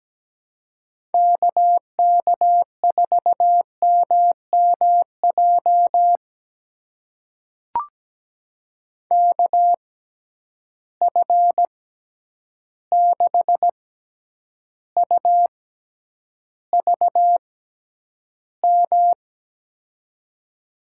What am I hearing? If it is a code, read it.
Morse code, heard as a keyed tone, about 17 wpm, spaced as for 9 wpm, 700 Hz. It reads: KK4MMJ KF6UVM
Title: USA Call Signs - 6 Characters - 6x Character Spacing - 17wpm